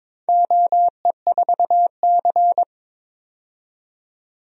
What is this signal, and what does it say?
Morse code, heard as a keyed tone, about 21 wpm, 700 Hz. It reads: OE4C
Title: DXCC-100 Call Signs - 4 Characters - 22wpm